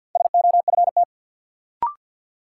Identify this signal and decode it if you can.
Morse code, heard as a keyed tone, about 48 wpm, 700 Hz. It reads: SOFT
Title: Sentences from Top 1500 Words - New Word Drill Alt - 50wpm